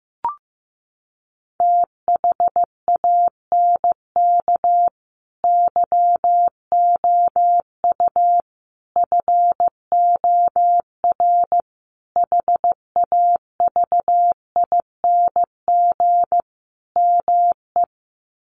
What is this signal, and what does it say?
Morse code, heard as a keyed tone, about 15 wpm, 700 Hz. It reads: THANK YOU FOR HAVING ME
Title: Sets of 5 Words Encore 15wpm